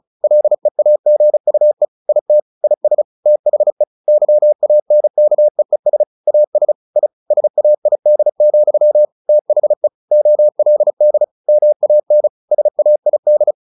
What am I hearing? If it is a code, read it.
Morse code, heard as a keyed tone, about 34 wpm, 600 Hz. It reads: LEAGUE IT IS THE YANKEES AS I SAID, THE OLD MAN SAID
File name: the old man and the sea / the old man and the sea 1 35wpm